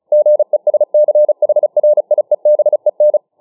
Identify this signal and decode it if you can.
Morse code, heard as a keyed tone, about 34 wpm, 600 Hz. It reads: GESCHRIEBEN